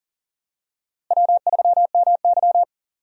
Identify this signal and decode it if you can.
Morse code, heard as a keyed tone, about 40 wpm, 700 Hz. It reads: W3MY
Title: USA Call Signs - 4 Characters - Repeated 3x - 40wpm